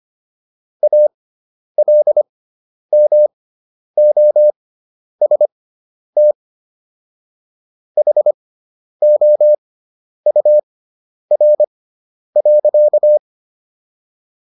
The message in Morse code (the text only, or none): ALMOST HOUR.